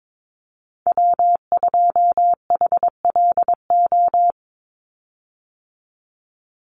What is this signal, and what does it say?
Morse code, heard as a keyed tone, about 22 wpm, 700 Hz. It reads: W2HLO
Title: USA Call Signs 22wpm